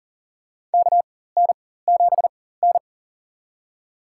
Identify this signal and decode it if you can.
Morse code, heard as a keyed tone, about 40 wpm, spaced as for 19 wpm, 700 Hz. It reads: KN7N
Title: USA Call Signs - 4 Characters - 4x Character Spacing - 40wpm